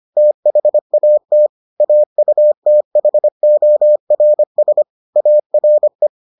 Morse code, heard as a keyed tone, about 25 wpm, 600 Hz. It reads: THAT AUTHORS ARE